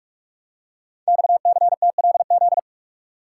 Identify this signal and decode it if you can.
Morse code, heard as a keyed tone, about 44 wpm, 690 Hz. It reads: XCTLZ
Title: Code Groups - 5 Letters - 45wpm